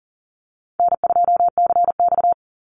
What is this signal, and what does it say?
Morse code, heard as a keyed tone, about 40 wpm, 700 Hz. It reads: N2CX
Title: Easy Call Signs - 40wpm